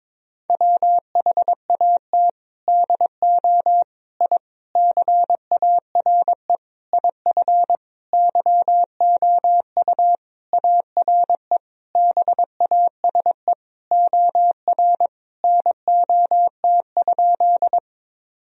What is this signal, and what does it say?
Morse code, heard as a keyed tone, about 22 wpm, 700 Hz. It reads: WHAT DO I CARE IF YOU ARE BASE OR NOT?
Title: Sentences from Top 1000 Words - Review Part 1 - 22wpm